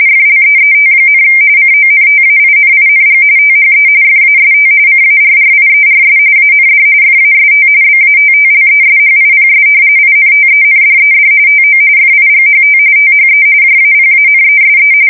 RTTY, about 45.45 baud. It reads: FLEET, CONTAINING THE FAST CARRIERS AND FAST BATTLESHIPS INTENDED TO FIGHT THE JAPANESE
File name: fsk170 / battleshipnj